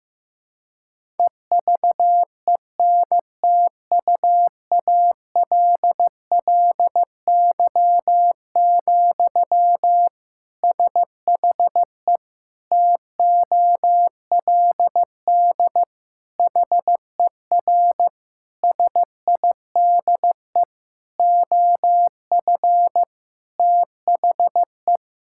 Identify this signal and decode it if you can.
Morse code, heard as a keyed tone, about 15 wpm, 700 Hz. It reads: EVENTUALLY, SHE TOLD HER SIDE OF THE
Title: Sentences from Top 1000 Words - Review Part 6 - 15wpm